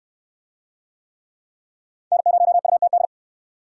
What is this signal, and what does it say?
Morse code, heard as a keyed tone, about 68 wpm, 690 Hz. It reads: N0UTD